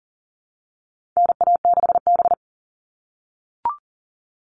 Morse code, heard as a keyed tone, about 40 wpm, 700 Hz. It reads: NA6B
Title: All Cut Numbers - 4-Digit 40wpm